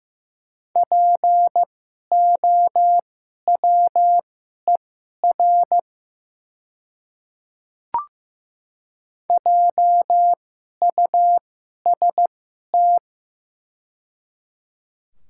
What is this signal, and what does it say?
Morse code, heard as a keyed tone, about 15 wpm, spaced as for 11 wpm, 700 Hz. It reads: POWER JUST
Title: 2x Character Spacing - Top 500 Words - 15wpm